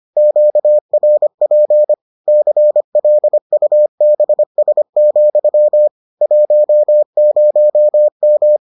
Morse code, heard as a keyed tone, about 25 wpm, 600 Hz. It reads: QRP CLUBS, 10M